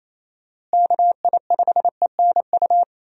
Morse code, heard as a keyed tone, about 28 wpm, 700 Hz. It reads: KI5ENU
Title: USA Call Signs - 6 Characters - Repeated 2x - 28wpm